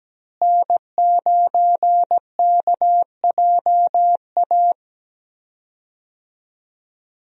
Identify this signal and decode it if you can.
Morse code, heard as a keyed tone, about 17 wpm, 700 Hz. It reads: N9KJA
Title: USA Call Signs - 5 Characters - Repeated 2x - 17wpm